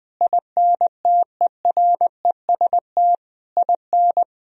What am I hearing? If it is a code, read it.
Morse code, heard as a keyed tone, about 20 wpm, 700 Hz. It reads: INTEREST IN
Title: Sentences from Top 1000 Words - Review Part 4 - 20wpm